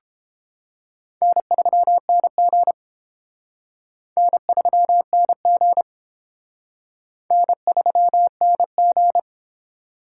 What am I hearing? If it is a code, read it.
Morse code, heard as a keyed tone, about 29 wpm, 700 Hz. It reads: N3NG N3NG N3NG